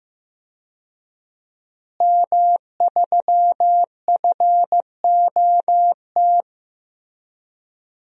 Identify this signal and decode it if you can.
Morse code, heard as a keyed tone, about 15 wpm, 700 Hz. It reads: M3FOT
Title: DXCC-100 Call Signs - 5 Characters - Repeated 3x - 15wpm